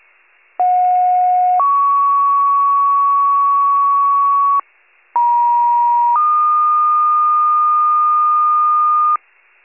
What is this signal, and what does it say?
Two-tone page, heard as a one-second tone, then a three-second tone, about 730 then 1120 Hz, about 950 then 1220 Hz. It is heard from 0.6 to 9.2 s.
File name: pager / 57 FIRE / wcalert